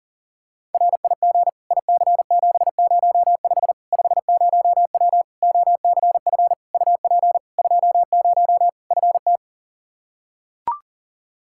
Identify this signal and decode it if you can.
Morse code, heard as a keyed tone, about 40 wpm, 700 Hz. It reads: RIG IC705 50W OCF UP 20 FT
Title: Practice QSO - Exchange 2 - Info - 40wpm